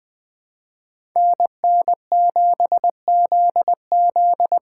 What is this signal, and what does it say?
Morse code, heard as a keyed tone, about 20 wpm, 700 Hz. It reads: NN7ZZ
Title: Speed Racing - USA 2x2 Call Signs 17wpm